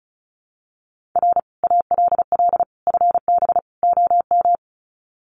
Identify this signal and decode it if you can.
Morse code, heard as a keyed tone, about 34 wpm, 700 Hz. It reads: R ALL FB OM
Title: Common QSO Phrases 35wpm